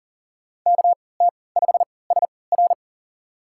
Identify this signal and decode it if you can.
Morse code, heard as a keyed tone, about 40 wpm, spaced as for 23 wpm, 700 Hz. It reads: KT5SR
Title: USA Call Signs - 3x Character Spacing - 40wpm